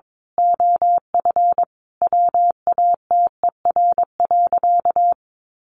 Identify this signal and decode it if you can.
Morse code, heard as a keyed tone, about 22 wpm, 700 Hz. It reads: OF WATER.
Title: Sentences from Top 1300 Words 22wpm